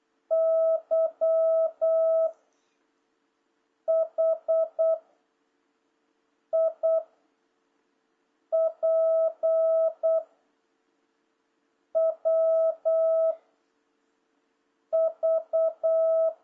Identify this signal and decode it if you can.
Morse code, heard as a keyed tone, about 8 wpm, 650 Hz. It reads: Y H I P W V
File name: jh8cbh / 2026cw